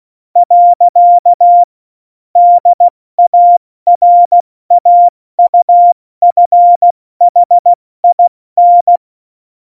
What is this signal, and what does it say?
Morse code, heard as a keyed tone, about 16 wpm, 700 Hz. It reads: . DARAUFHIN
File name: Morsekurs / Witze / Witz 22. Januar 2019, 70 F80